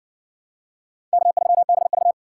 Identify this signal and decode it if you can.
Morse code, heard as a keyed tone, about 60 wpm, 700 Hz. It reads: K3BV